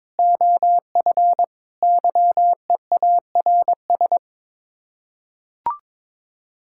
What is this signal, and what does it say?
Morse code, heard as a keyed tone, about 22 wpm, 700 Hz. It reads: OF YEARS
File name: practice / 22wpm / Sets of 6 Words - 22wpm